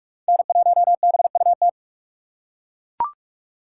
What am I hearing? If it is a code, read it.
Morse code, heard as a keyed tone, about 41 wpm, 700 Hz. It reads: N1BUT
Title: USA Call Signs - 3x Character Spacing - 45wpm